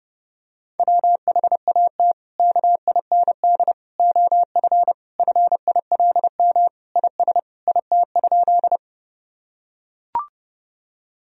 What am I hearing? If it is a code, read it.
Morse code, heard as a keyed tone, about 30 wpm, 700 Hz. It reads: WHAT KIND OF FILM IS IT?
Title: Sentences from Top 1000 Words - Review Part 3 - 30wpm